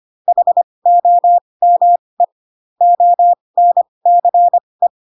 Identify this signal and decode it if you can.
Morse code, heard as a keyed tone, about 25 wpm, spaced as for 20 wpm, 700 Hz. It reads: HOME ONCE
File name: Pinocchio / Pinocchio ch8 15 25 ik1hge